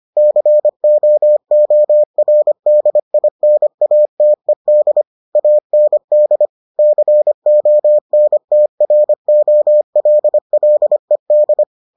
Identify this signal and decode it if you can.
Morse code, heard as a keyed tone, about 25 wpm, 600 Hz. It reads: COORDINATED AND CONTROLLED